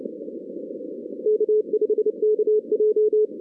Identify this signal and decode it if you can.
Morse code, heard as a keyed tone, about 29 wpm, 440 Hz. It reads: K5KJ
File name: 2016 arrl dx cw / 2016-ARRL-DX-CW-K5TR- - 0-41